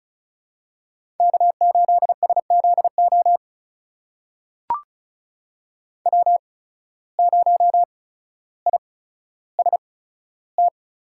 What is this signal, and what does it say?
Morse code, heard as a keyed tone, about 36 wpm, spaced as for 15 wpm, 700 Hz. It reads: K8SZO W0IST
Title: USA Call Signs - 5 Characters - 8x Character Spacing - 35wpm